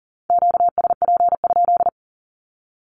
Morse code, heard as a keyed tone, about 38 wpm, 700 Hz. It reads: QSP?
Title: Q-Codes with Spoken Meaning - 40wpm